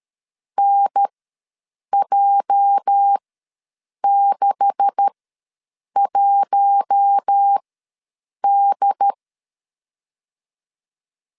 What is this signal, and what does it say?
Morse code, heard as a keyed tone, about 13 wpm, spaced as for 7 wpm, 800 Hz. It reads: NJ61D
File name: LEZIONI / 10 SIGNS / 10C DOT ws 12wpm 3548